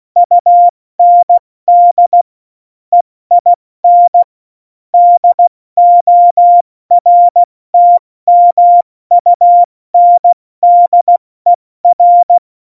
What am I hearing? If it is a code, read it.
Morse code, heard as a keyed tone, about 16 wpm, 700 Hz. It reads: UND EIN DORTMUNDER
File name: Morsekurs / Witze / Witz 12. Januar 2021, 70 F80